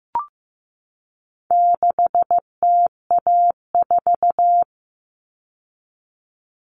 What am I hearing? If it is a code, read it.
Morse code, heard as a keyed tone, about 15 wpm, 700 Hz. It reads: 6TA4
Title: All Cut Numbers - 4-Digit 15wpm